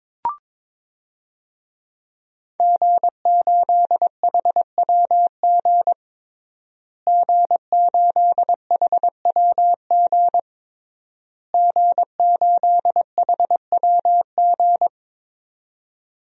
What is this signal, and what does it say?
Morse code, heard as a keyed tone, about 22 wpm, 700 Hz. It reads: G8HWG G8HWG G8HWG
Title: DXCC-100 Call Signs - 5 Characters - Repeated 3x - 22wpm